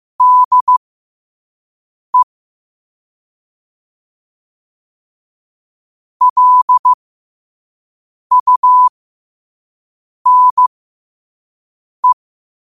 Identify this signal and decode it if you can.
Morse code, heard as a keyed tone, about 15 wpm, spaced as for 5 wpm, 1000 Hz. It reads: DE LUNE
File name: qotdfiles / 5wpm